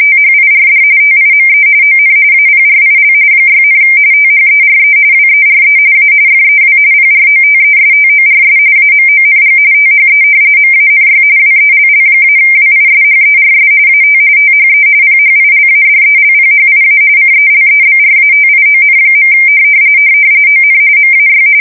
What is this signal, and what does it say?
RTTY, about 45.45 baud. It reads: THIRD TIME ON APRIL 6, THERE WERE A FEW PROTESTORS AMONG THE MANY WELL-WISHERS AT THE PHILADELPHIA CELEBRATION. ON MAY
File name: fsk170 / battleshipnj